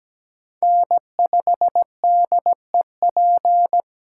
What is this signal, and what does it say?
Morse code, heard as a keyed tone, about 20 wpm, 700 Hz. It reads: N5DEP
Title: Speed Racing - USA 1x3 Call Signs 17wpm